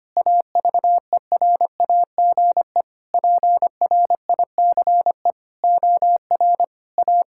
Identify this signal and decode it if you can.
Morse code, heard as a keyed tone, about 25 wpm, 700 Hz. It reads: AVERAGE PRICE OR A
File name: Sets of 9 Words - 25wpm